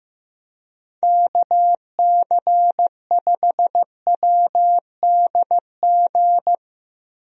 Morse code, heard as a keyed tone, about 15 wpm, 700 Hz. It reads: KC5WDG
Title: USA Call Signs - 6 Characters - Repeated 3x - 15wpm